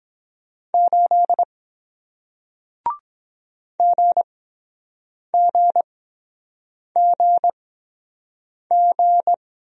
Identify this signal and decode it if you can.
Morse code, heard as a keyed tone, about 23 wpm, spaced as for 7 wpm, 700 Hz. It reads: 8 GGGG